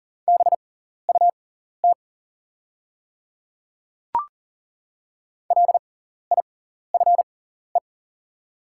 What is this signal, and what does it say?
Morse code, heard as a keyed tone, about 40 wpm, spaced as for 14 wpm, 700 Hz. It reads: BUT LIFE
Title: 6x Character Spacing - Top 300 Words - 40wpm